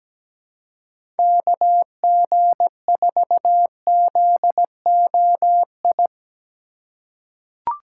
Morse code, heard as a keyed tone, about 17 wpm, 700 Hz. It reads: KG4ZOI
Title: USA Call Signs - 6 Characters - 4x Character Spacing - 17wpm